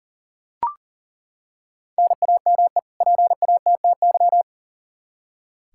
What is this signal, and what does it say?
Morse code, heard as a keyed tone, about 40 wpm, 700 Hz. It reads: NAME PATTY
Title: K1USN SST QSO Practice 40wpm